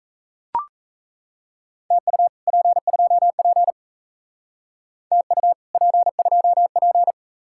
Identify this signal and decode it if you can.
Morse code, heard as a keyed tone, about 32 wpm, 700 Hz. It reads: TU P2P TU P2P
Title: Speed-Racing Common QSO Phrases - 28wpm